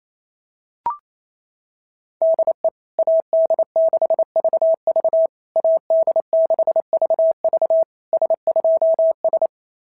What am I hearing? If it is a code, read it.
Morse code, heard as a keyed tone, about 28 wpm, 650 Hz. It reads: DE AD6VV AD6VV S2S
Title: SOTA QSO Practice 28wpm